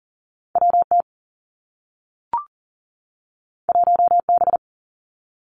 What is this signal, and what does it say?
Morse code, heard as a keyed tone, about 40 wpm, 700 Hz. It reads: W T 1 B
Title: Two Challenging Letter-Number Combinations 40wpm